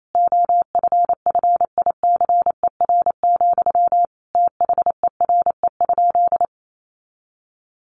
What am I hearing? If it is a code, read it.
Morse code, heard as a keyed tone, about 28 wpm, 700 Hz. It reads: OFFICER, THERE?
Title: Sentences from Top 1200 Words 28wpm